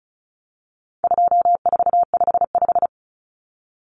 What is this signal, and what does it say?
Morse code, heard as a keyed tone, about 34 wpm, 700 Hz. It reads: 2455
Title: Common Cut Numbers - 4-Digit 35wpm